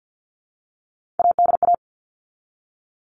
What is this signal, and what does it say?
Morse code, heard as a keyed tone, about 50 wpm, 700 Hz. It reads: ADA
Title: All Cut Numbers - 3-Digit 50wpm